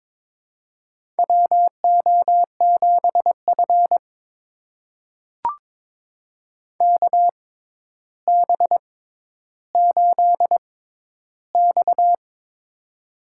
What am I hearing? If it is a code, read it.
Morse code, heard as a keyed tone, about 22 wpm, spaced as for 11 wpm, 700 Hz. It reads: WO7F KB8X
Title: USA Call Signs - 4 Characters - 6x Character Spacing - 22wpm